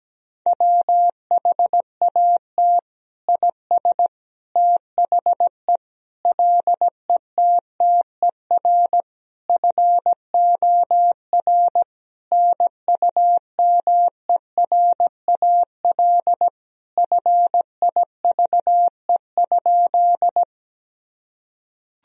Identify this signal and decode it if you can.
Morse code, heard as a keyed tone, about 17 wpm, 700 Hz. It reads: WHAT IS THE LETTER FOR NUMERAL FIVE?
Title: Sentences from Top 1000 Words - Review Part 2 - 17wpm